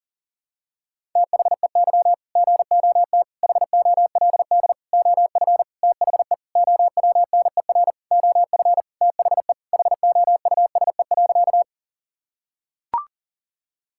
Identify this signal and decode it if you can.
Morse code, heard as a keyed tone, about 37 wpm, 700 Hz. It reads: THEY GOT HOLD OF THE OWNER OF THE HOUSE.
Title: Sentences from Top 1600 Words 40wpm